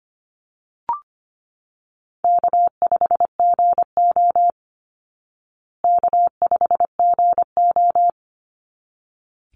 Morse code, heard as a keyed tone, about 25 wpm, 700 Hz. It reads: K5GO K5GO
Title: Easy Call Signs Repeated 2x - 25wpm